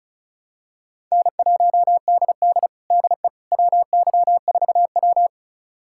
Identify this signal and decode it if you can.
Morse code, heard as a keyed tone, about 36 wpm, 700 Hz. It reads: N1DD DE WY4W
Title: USA Callsign DE Callsign - 4 Characters - 35wpm